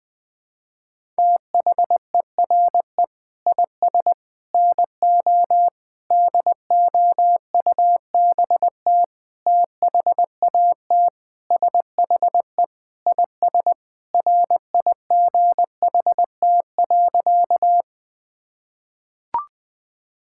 Morse code, heard as a keyed tone, about 20 wpm, 700 Hz. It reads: THERE IS NO DOUBT THAT SHE IS RIGHT.
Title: Sentences with 5-Word Phrases - 20wpm